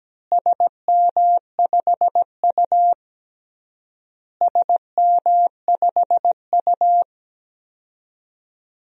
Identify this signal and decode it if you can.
Morse code, heard as a keyed tone, about 17 wpm, 700 Hz. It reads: SM5U SM5U
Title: DXCC-100 Call Signs - 4 Characters - Repeated 3x - 17wpm